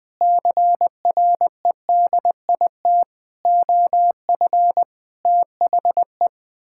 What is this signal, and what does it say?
Morse code, heard as a keyed tone, about 20 wpm, 700 Hz. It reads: CREDIT OF THE
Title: Sentences with 9-Word Phrases - 20wpm